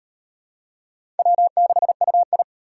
Morse code, heard as a keyed tone, about 35 wpm, 700 Hz. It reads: W6UI